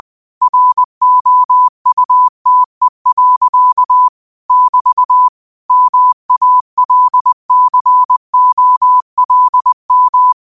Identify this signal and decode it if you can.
Morse code, heard as a keyed tone, about 20 wpm, 1000 Hz. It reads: ROUTE. <BT> MALCOLM